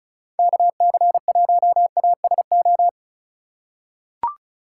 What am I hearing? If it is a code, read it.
Morse code, heard as a keyed tone, about 36 wpm, 700 Hz. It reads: KC1ASO
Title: USA Call Signs - 3x Character Spacing - 35wpm